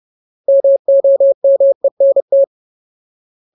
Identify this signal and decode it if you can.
Morse code, heard as a keyed tone, about 30 wpm, 550 Hz. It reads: MOMENT
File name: Ultimate Word List / 30WPM Ultimate